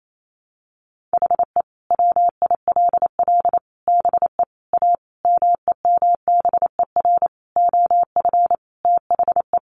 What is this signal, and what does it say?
Morse code, heard as a keyed tone, about 28 wpm, 700 Hz. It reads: HE WILL BE A MEMBER OF THE
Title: Sentences from Top 1100 Words 28wpm